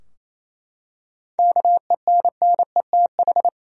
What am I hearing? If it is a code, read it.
Morse code, heard as a keyed tone, about 27 wpm, 700 Hz. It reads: KENNETH
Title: USA Common Names - 28wpm